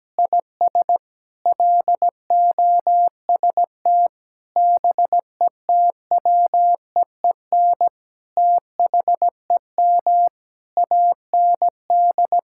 Morse code, heard as a keyed tone, about 17 wpm, 700 Hz. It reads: IS LOST BETWEEN THEM AND
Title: Sentences from Top 1600 Words 17wpm